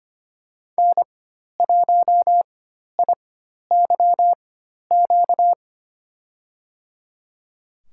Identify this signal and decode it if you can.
Morse code, heard as a keyed tone, about 25 wpm, spaced as for 12 wpm, 700 Hz. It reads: N1IYQ
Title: USA Call Signs - 4x Character Spacing - 25wpm